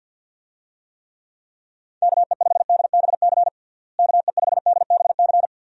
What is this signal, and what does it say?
Morse code, heard as a keyed tone, about 50 wpm, 700 Hz. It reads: KE5DBC KE5DBC
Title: USA Call Signs - 6 Characters - Repeated 2x - 50wpm